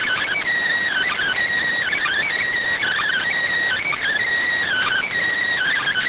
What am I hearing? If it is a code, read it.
Emergency Alert System (SAME) tones (header bursts), heard as harsh buzzing bursts. It is heard from 0.9 to 5.1 s.